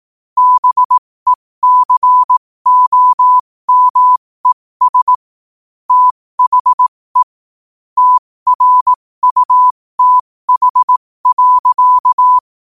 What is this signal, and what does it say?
Morse code, heard as a keyed tone, about 18 wpm, spaced as for 15 wpm, 1000 Hz. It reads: BECOMES THE TRUTH.
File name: qotdfiles / 15wpm